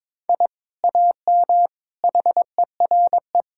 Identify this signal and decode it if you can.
Morse code, heard as a keyed tone, about 22 wpm, 700 Hz. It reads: I AM HERE
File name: practice / 22wpm / Sentences with 2-Word Phrases - 22wpm